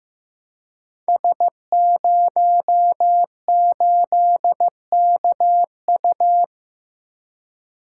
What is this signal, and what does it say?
Morse code, heard as a keyed tone, about 15 wpm, 700 Hz. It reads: S08KU
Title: Code Groups - 5 Letter:Numbers - 15wpm